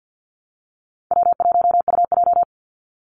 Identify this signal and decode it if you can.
Morse code, heard as a keyed tone, about 49 wpm, 700 Hz. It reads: W1UJ
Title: Call Signs Repeated 3x - 50wpm